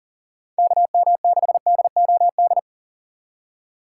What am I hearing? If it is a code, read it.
Morse code, heard as a keyed tone, about 37 wpm, 690 Hz. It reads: KM6DOD
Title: USA Call Signs - 4x Character Spacing - 40wpm